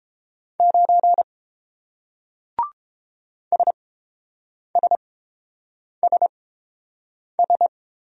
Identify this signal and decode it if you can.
Morse code, heard as a keyed tone, about 30 wpm, spaced as for 7 wpm, 700 Hz. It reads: 9 SSSS